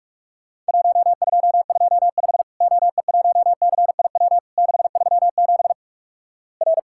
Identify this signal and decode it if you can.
Morse code, heard as a keyed tone, about 44 wpm, 700 Hz. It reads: 1225 OE1CIW 637 R
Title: Worked All Europe DX Contest - QTC Exchange - 45wpm